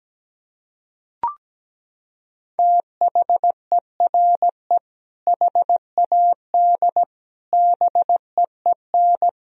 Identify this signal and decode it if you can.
Morse code, heard as a keyed tone, about 17 wpm, 700 Hz. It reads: THERE HAD BEEN
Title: Sentences from Top 1000 Words - Review Part 1 - 17wpm